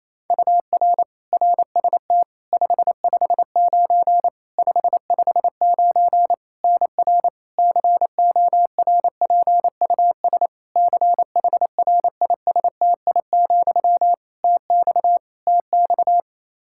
Morse code, heard as a keyed tone, about 28 wpm, 700 Hz. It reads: UR RST 559 559 NR CORPUS CHRISTI, TX TX
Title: K1USN SST QSO Practice 28wpm